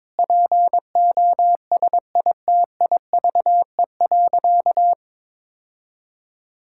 Morse code, heard as a keyed tone, about 22 wpm, 700 Hz. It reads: POSITIVE.
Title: Sentences from Top 1000 Words - Review Part 5 - 22wpm